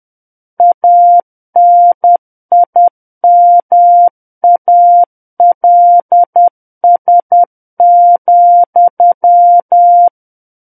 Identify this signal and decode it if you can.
Morse code, heard as a keyed tone, about 10 wpm, 700 Hz. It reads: ANIMALS,
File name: Book One / 01-Chapter 1 - 10 WPM